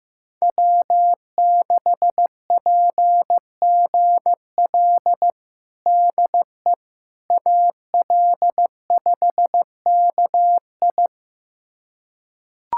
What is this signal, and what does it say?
Morse code, heard as a keyed tone, about 15 wpm, 700 Hz. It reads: W6PGL DE AL5KI
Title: USA Callsign DE Callsign - 5 Characters - 15wpm